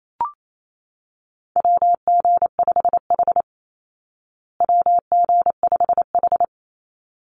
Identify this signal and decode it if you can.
Morse code, heard as a keyed tone, about 28 wpm, 700 Hz. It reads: WG5H WG5H
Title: Call Signs Repeated 3x - 28wpm